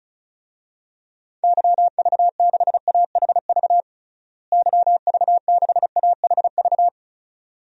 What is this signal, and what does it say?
Morse code, heard as a keyed tone, about 36 wpm, 700 Hz. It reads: YV6AHV YV6AHV
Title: DXCC-100 Call Signs - 6 Characters - Repeated 2x - 35wpm